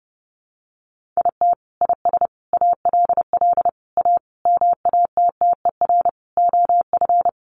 Morse code, heard as a keyed tone, about 30 wpm, 700 Hz. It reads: IT IS ALL A MATTER OF
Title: Sentences from Top 1300 Words 30wpm